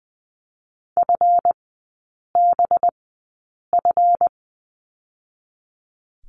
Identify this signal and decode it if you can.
Morse code, heard as a keyed tone, about 20 wpm, spaced as for 8 wpm, 700 Hz. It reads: FBF
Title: 184 Lesson 30e - Letter b - Three Characters 20wpm